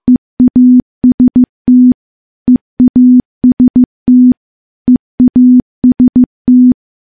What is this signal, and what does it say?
Morse code, heard as a keyed tone, about 15 wpm, 250 Hz. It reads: EAST EAST EAST
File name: morse-training-15WPM